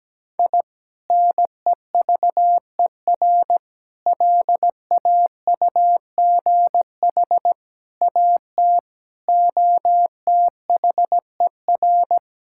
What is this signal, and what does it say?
Morse code, heard as a keyed tone, about 17 wpm, 700 Hz. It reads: I NEVER LAUGH AT OTHER
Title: Sentences from Top 1000 Words - Review Part 6 - 17wpm